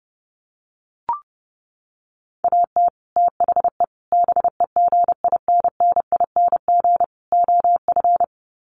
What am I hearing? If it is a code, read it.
Morse code, heard as a keyed tone, about 28 wpm, 700 Hz. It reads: AT THE BEGINNING OF
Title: Sets of 4 Words 30wpm